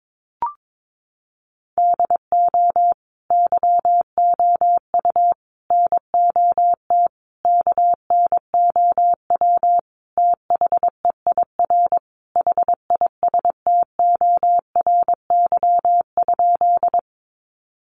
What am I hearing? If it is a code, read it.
Morse code, heard as a keyed tone, about 22 wpm, 700 Hz. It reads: DO YOU NOT KNOW THEIR HISTORY?
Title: Sentences from Top 600 Words 22wpm